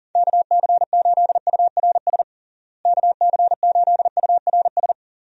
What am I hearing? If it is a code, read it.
Morse code, heard as a keyed tone, about 40 wpm, 700 Hz. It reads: KC8URS KC8URS
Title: USA Call Signs - 6 Characters - Repeated 3x - 40wpm